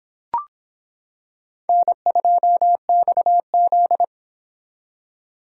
Morse code, heard as a keyed tone, about 24 wpm, 700 Hz. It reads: N2XZ